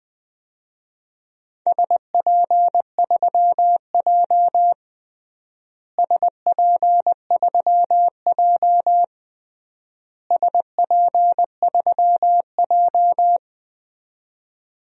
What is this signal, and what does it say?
Morse code, heard as a keyed tone, about 20 wpm, 700 Hz. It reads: SP3J SP3J SP3J
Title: DXCC-100 Call Signs - 4 Characters - Repeated 3x - 20wpm